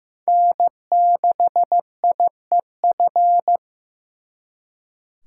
Morse code, heard as a keyed tone, about 18 wpm, 700 Hz. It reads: N6IEF
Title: Speed Racing - USA Call Signs 15wpm